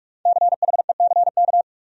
Morse code, heard as a keyed tone, about 55 wpm, 700 Hz. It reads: CHECK
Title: Speed-Racing Top 500 Words 45wpm